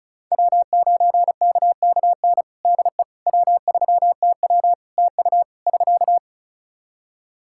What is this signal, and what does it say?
Morse code, heard as a keyed tone, about 34 wpm, 700 Hz. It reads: W9KKN DE W3TW TU <SK>
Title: Practice QSO - Exchange 4 - Ending - 35wpm